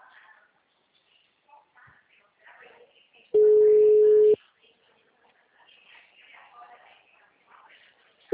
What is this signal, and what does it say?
Ringback tone (Europe, 425 Hz), with speech.